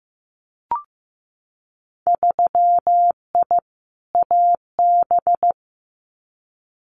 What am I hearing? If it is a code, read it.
Morse code, heard as a keyed tone, about 15 wpm, 700 Hz. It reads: 3I AB
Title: Winter Field Day Exchanges Encore 15wpm